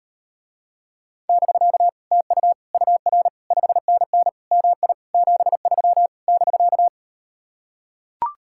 Morse code, heard as a keyed tone, about 38 wpm, 700 Hz. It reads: <BK> TU UR 5NN MI 73 <BK>